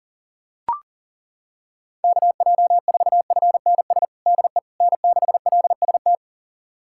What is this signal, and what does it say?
Morse code, heard as a keyed tone, about 40 wpm, 700 Hz. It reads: KJ4FNS DE N6LST
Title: USA Callsign DE Callsign 40wpm